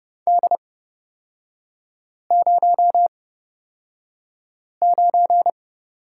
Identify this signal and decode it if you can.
Morse code, heard as a keyed tone, about 29 wpm, spaced as for 5 wpm, 690 Hz. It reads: D09
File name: course / 30:10wpm / 236 Lesson 38e - Letter q - Three Characters 30wpm